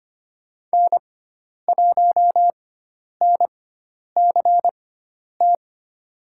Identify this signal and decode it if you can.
Morse code, heard as a keyed tone, about 25 wpm, spaced as for 10 wpm, 700 Hz. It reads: N1NCT
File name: Lesson 16g - Letter c - Call Signs 25wpm